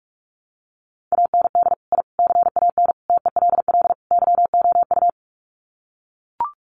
Morse code, heard as a keyed tone, about 44 wpm, 700 Hz. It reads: AND I CAN TELL YOU
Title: Sets of 5 Words Encore2 45wpm